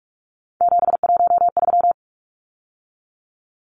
Morse code, heard as a keyed tone, about 44 wpm, 700 Hz. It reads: 713
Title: Numbers - 3-Digit 45wpm